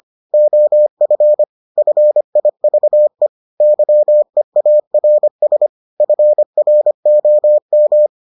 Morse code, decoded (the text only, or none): OF FIVE YEARS FROM